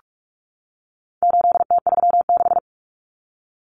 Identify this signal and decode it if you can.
Morse code, heard as a keyed tone, about 44 wpm, 700 Hz. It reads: 8T36
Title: Common Cut Numbers - 4-Digit 45wpm